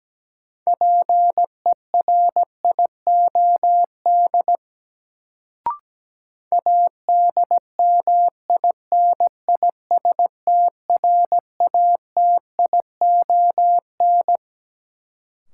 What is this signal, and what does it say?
Morse code, heard as a keyed tone, about 17 wpm, 700 Hz. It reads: PERIOD ADMINISTRATION
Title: Sets of 5 Words - New Word Drill - 17wpm